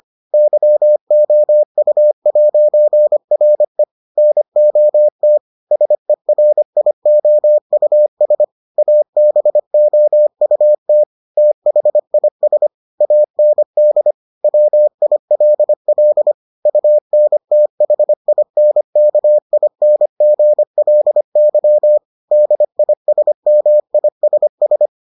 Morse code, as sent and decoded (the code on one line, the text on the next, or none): -.-- --- ..- .----. .-. / -. --- - / ... .-. .. --- ..- ... / .- -... --- ..- - / - .... .. ... / .- -. -.. / .-- .. .-.. .-.. / ..- -. - .... .. -. -.- .. -. --. .-.. -.-- / -.. .. ... -- .. ... ...
YOU'RE NOT SERIOUS ABOUT THIS AND WILL UNTHINKINGLY DISMISS